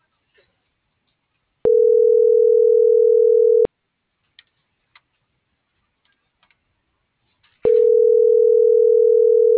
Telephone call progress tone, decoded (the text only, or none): ringback tone